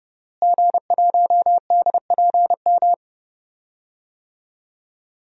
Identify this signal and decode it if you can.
Morse code, heard as a keyed tone, about 30 wpm, 700 Hz. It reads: G1DPM
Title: DXCC-100 Call Signs - 5 Characters - Repeated 3x - 30wpm